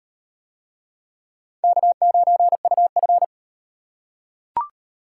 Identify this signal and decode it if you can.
Morse code, heard as a keyed tone, about 38 wpm, 700 Hz. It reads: K9UF